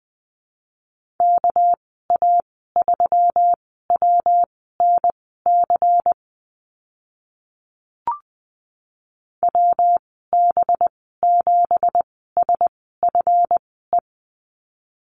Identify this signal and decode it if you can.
Morse code, heard as a keyed tone, about 20 wpm, spaced as for 14 wpm, 700 Hz. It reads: KA3WNC WB7SFE
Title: 213 Lesson 34g - Number 8 - Call Signs 20wpm